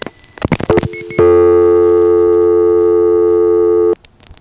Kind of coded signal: telephone call progress tone